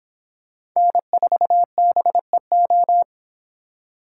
Morse code, heard as a keyed tone, about 26 wpm, 700 Hz. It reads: N4BEO